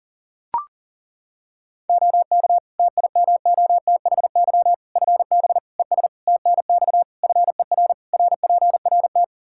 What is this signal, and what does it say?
Morse code, heard as a keyed tone, about 40 wpm, 700 Hz. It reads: OK TIMOTHY FB ES TNX FER RPRT
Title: Practice QSO - Exchange 2 - Info - 40wpm